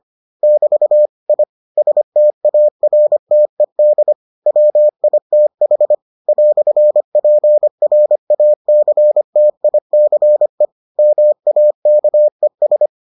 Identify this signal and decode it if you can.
Morse code, heard as a keyed tone, about 25 wpm, 600 Hz. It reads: <BT> I STARTED WITH "PRACTICE MAKES